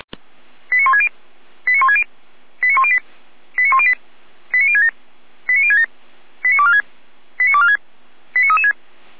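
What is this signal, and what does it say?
Five-tone selective call (ZVEI), heard as quick runs of five tones, tones of about 70 ms. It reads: 89160 89160 89108 89108 89067 89067 89246 89246 89336
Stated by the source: ZVEI five-tone selective call received on an SDR.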